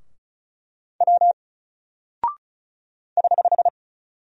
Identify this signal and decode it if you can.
Morse code, heard as a keyed tone, about 36 wpm, 700 Hz. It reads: W <HH>
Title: Single Character (Letter:Number:Punctuation:Prosign) Alt 35wpm